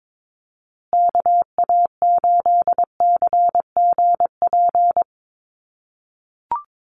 Morse code, heard as a keyed tone, about 22 wpm, 700 Hz. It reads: KA8CGP
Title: USA Call Signs Repeated 2x 22wpm